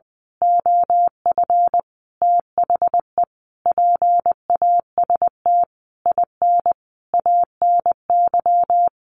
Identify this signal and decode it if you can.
Morse code, heard as a keyed tone, about 19 wpm, 700 Hz. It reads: OF THE PAST IN ANY
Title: A Sound Of Thunder 20wpm